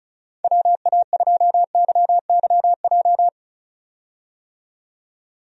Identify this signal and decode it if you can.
Morse code, heard as a keyed tone, about 36 wpm, 700 Hz. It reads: WA2YYJ
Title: USA Call Signs - 6 Characters - 35wpm